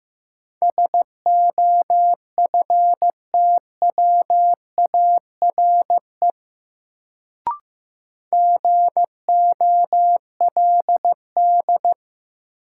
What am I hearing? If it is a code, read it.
Morse code, heard as a keyed tone, about 15 wpm, 700 Hz. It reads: SOFTWARE GOLD
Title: Sentences from Top 1000 Words - Review Part 3 - New Word Drill - 15wpm